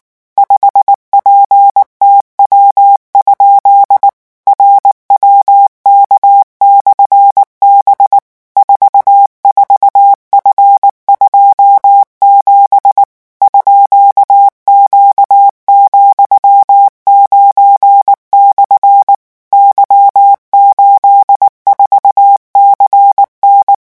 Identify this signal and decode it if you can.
Morse code, heard as a keyed tone, about 19 wpm, 800 Hz. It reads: 5PTW? RWK/B 44F27 _Q,9/ Y84CN
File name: morse mp3 / revision13-vitesse 18 mots